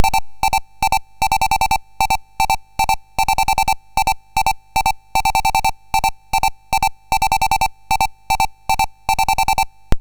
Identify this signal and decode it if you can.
There is no code.